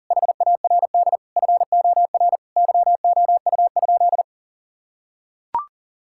Morse code, heard as a keyed tone, about 37 wpm, 700 Hz. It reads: HARD FOR YOU?
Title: Sentences from Top 1000 Words - Review Part 6 - 40wpm